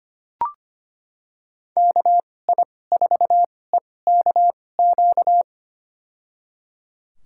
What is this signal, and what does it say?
Morse code, heard as a keyed tone, about 25 wpm, spaced as for 18 wpm, 700 Hz. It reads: KI4EKQ
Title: USA Call Signs - 6 Characters - 2x Character Spacing - 25wpm